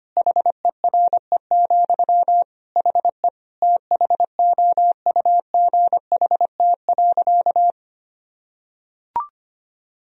Morse code, heard as a keyed tone, about 25 wpm, 700 Hz. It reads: HERE, HE THOUGHT.
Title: Sentences from Top 1000 Words - Review Part 3 - 25wpm